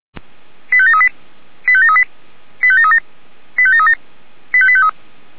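Five-tone selective call (ZVEI), heard as quick runs of five tones, tones of about 70 ms. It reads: 85738 85738 85737 85737 85853
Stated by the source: Five-tone ZVEI selective call received on an SDR.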